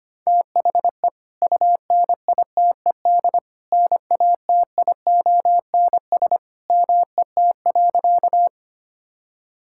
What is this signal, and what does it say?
Morse code, heard as a keyed tone, about 25 wpm, 700 Hz. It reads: THE UNITED NATIONS MET.
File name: practice / 25wpm / Sentences with 2-Word Phrases - 25wpm